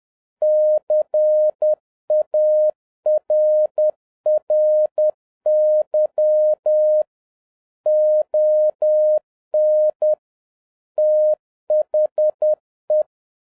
Morse code, heard as a keyed tone, about 10 wpm, 610 Hz. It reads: CARRY ON THE